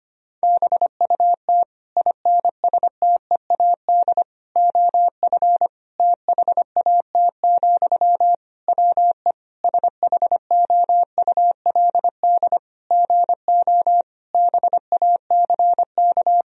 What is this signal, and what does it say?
Morse code, heard as a keyed tone, about 25 wpm, 700 Hz. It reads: BUT INSTEAD OF THAT, WE SHOULD GO BACK
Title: Sentences from Top 1000 Words - Review Part 4 - 25wpm